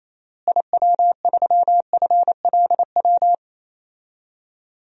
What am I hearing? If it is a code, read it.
Morse code, heard as a keyed tone, about 28 wpm, 700 Hz. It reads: IW3FLW
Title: DXCC-100 Call Signs - 6 Characters - Repeated 3x - 28wpm